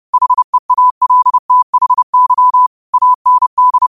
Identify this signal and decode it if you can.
Morse code, heard as a keyed tone, about 30 wpm, 1000 Hz. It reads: HEALTHY AND
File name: qotdfiles / 30wpm